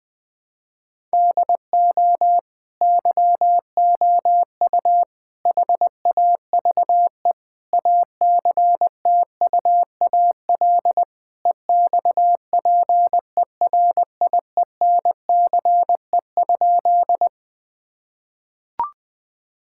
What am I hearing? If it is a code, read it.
Morse code, heard as a keyed tone, about 20 wpm, 700 Hz. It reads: DO YOU HAVE ACTUAL EXPERIENCE?
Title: Sentences from Top 1000 Words - Review Part 5 - 20wpm